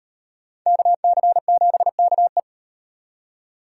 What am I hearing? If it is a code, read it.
Morse code, heard as a keyed tone, about 38 wpm, 700 Hz. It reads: KC7KE